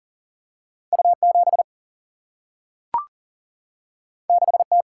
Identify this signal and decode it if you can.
Morse code, heard as a keyed tone, about 40 wpm, 700 Hz. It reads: U 7 6 T
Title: Single-Letter Single-Number 40wpm